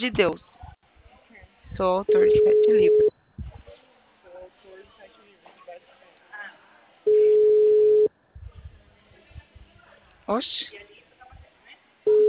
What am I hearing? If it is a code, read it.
Ringback tone (Europe, 425 Hz), with speech.
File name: out- -2006- - - .0